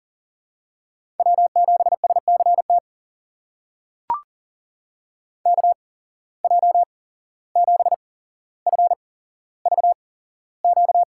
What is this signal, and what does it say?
Morse code, heard as a keyed tone, about 40 wpm, spaced as for 16 wpm, 700 Hz. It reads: W7SCT KJ7FVQ
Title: USA Call Signs - 8x Character Spacing - 40wpm